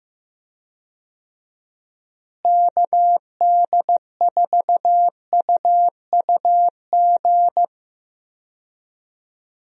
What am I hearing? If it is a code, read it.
Morse code, heard as a keyed tone, about 15 wpm, 700 Hz. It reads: KD4UUG
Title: USA Call Signs - 6 Characters - Repeated 2x - 15wpm